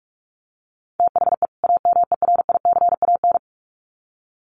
Morse code, heard as a keyed tone, about 44 wpm, 700 Hz. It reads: THE AMERICAN
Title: Sets of 2 Words 45wpm